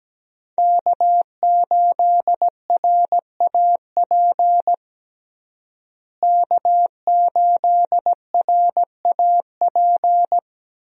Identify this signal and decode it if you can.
Morse code, heard as a keyed tone, about 17 wpm, 700 Hz. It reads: K8RAP K8RAP
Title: USA Call Signs - 5 Characters - Repeated 3x - 17wpm